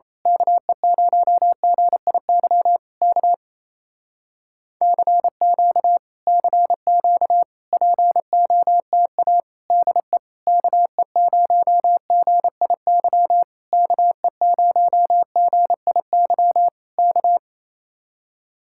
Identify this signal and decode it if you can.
Morse code, heard as a keyed tone, about 27 wpm, 700 Hz. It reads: KE0GIY K CQ CQ POTA DE KE0GIY KE0GIY K
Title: Speed-Racing Common QSO Phrases Encore - 28wpm